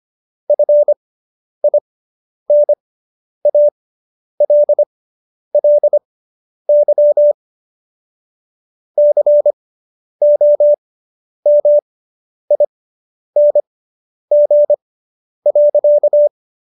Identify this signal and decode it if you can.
Morse code, heard as a keyed tone, about 25 wpm, spaced as for 10 wpm, 590 Hz. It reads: FINALLY COMING.